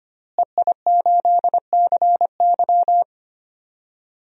Morse code, heard as a keyed tone, about 25 wpm, 700 Hz. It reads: EI8CY
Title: DXCC-100 Call Signs - 5 Characters - Repeated 3x - 25wpm